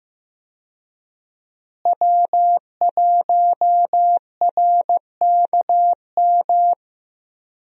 Morse code, heard as a keyed tone, about 15 wpm, 700 Hz. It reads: W1RKM
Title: USA Call Signs - 5 Characters - Repeated 2x - 15wpm